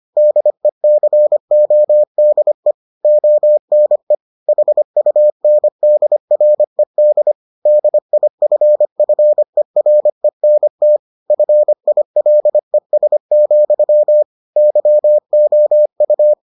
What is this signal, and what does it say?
Morse code, heard as a keyed tone, about 25 wpm, 600 Hz. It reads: DECODE ONE HUNDRED DIFFERENT FILES, YOU